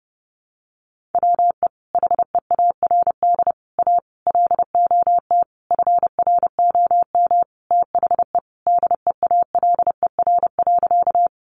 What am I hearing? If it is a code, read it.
Morse code, heard as a keyed tone, about 30 wpm, 700 Hz. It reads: WE HEARD A LOT FROM THE DEALER.
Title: Sentences from Top 1100 Words 30wpm